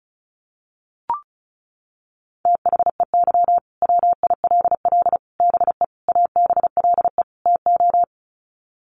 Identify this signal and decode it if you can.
Morse code, heard as a keyed tone, about 34 wpm, 700 Hz. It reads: THEY WILL BE ABLE TO
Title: Sets of 5 Words Encore 35wpm